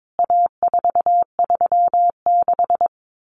Morse code, heard as a keyed tone, about 22 wpm, 700 Hz. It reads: A436
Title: CWT Exchange 22wpm